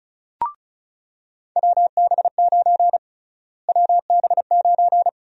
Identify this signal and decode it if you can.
Morse code, heard as a keyed tone, about 36 wpm, 700 Hz. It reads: WB9 WB9
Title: USA Call Sign Prefixes - Repeated 2x - 35wpm